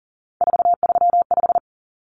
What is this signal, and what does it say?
Morse code, heard as a keyed tone, about 40 wpm, 700 Hz. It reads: 435
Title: Common Cut Numbers - 3-Digit 40wpm